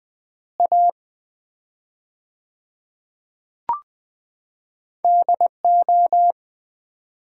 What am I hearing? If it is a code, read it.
Morse code, heard as a keyed tone, about 21 wpm, 690 Hz. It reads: A DO
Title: 4x Word-Spacing - Sets of 2 Words - 20wpm